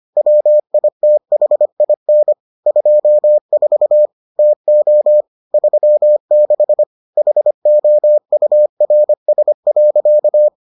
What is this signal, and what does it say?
Morse code, heard as a keyed tone, about 25 wpm, 600 Hz. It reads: WITHIN 24 TO 36 HOURS.